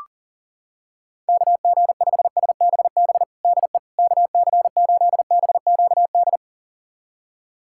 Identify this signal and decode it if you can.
Morse code, heard as a keyed tone, about 40 wpm, 700 Hz. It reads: KG5SBB DE KC8BQD
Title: USA Callsign DE Callsign - 6 Characters - 40wpm